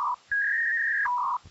Fax machine handshake.